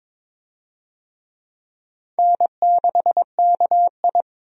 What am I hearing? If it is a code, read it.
Morse code, heard as a keyed tone, about 21 wpm, 700 Hz. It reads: N6KI
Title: USA Call Signs - 4 Characters - 22wpm